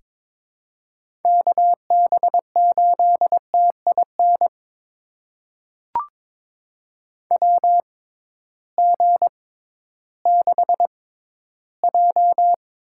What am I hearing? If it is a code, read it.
Morse code, heard as a keyed tone, about 21 wpm, spaced as for 13 wpm, 700 Hz. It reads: KB8TIN WG6J
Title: USA Call Signs - 6x Character Spacing - 22wpm